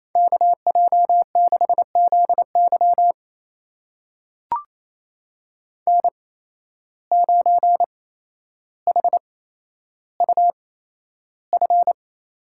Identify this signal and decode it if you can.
Morse code, heard as a keyed tone, about 28 wpm, spaced as for 12 wpm, 700 Hz. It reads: KJ6ZY N9HUF
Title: USA Call Signs - 5 Characters - 8x Character Spacing - 28wpm